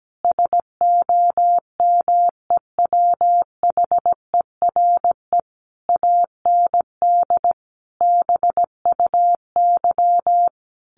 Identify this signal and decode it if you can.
Morse code, heard as a keyed tone, about 17 wpm, 700 Hz. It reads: SOMEWHERE AND BUY
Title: Sentences from Top 1400 Words 17wpm